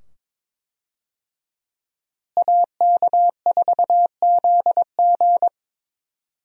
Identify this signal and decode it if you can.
Morse code, heard as a keyed tone, about 22 wpm, 700 Hz. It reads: AK4ZG
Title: USA Call Signs - 5 Characters - Repeated 2x - 22wpm